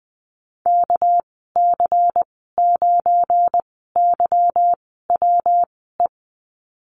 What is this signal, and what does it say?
Morse code, heard as a keyed tone, about 20 wpm, spaced as for 15 wpm, 700 Hz. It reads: KC9YWE
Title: 213 Lesson 34g - Number 8 - Call Signs 20wpm